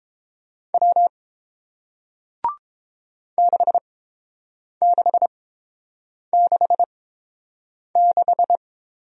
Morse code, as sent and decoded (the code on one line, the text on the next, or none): .-- / -.... -.... -.... -....
W 6666